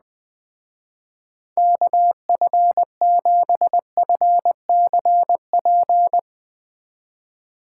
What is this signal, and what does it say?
Morse code, heard as a keyed tone, about 20 wpm, 700 Hz. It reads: KF7FCP
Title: Speed Racing - USA Call Signs 17wpm